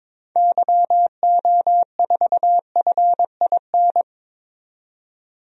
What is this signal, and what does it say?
Morse code, heard as a keyed tone, about 22 wpm, 700 Hz. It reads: YO4FIN
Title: DXCC-100 Call Signs - 6 Characters - 22wpm